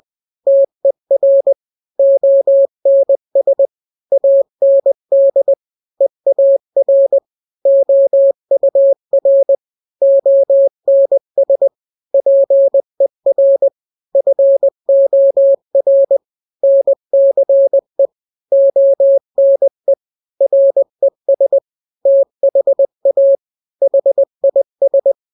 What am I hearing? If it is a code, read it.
Morse code, heard as a keyed tone, about 20 wpm, 550 Hz. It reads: TER ONS AND EAR OUR ONS PER FOR NCE ONE RES THA HIS
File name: n-grams / 3-grams 20 18 WPM